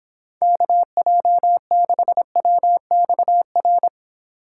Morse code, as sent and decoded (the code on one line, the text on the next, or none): -.- .--- -.... .-- -..- .-.
KJ6WXR